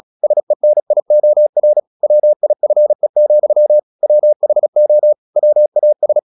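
Morse code, heard as a keyed tone, about 36 wpm, 600 Hz. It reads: SENIOR WIFE, WHO WAS
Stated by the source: Morse practice at 36 wpm, text of Ken Follett's novel Never, from CW med Gnister.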